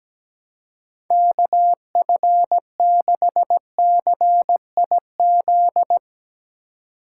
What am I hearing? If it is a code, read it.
Morse code, heard as a keyed tone, about 17 wpm, 700 Hz. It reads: KF6CIZ
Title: USA Call Signs - 6 Characters - 3x Character Spacing - 17wpm